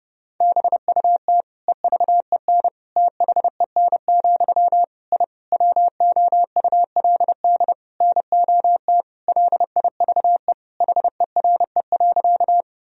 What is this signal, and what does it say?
Morse code, heard as a keyed tone, about 30 wpm, 700 Hz. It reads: BUT EVEN THEN, I WOULD NOT LIVE HERE.
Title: Sentences from Top 1000 Words - Review Part 1 - 30wpm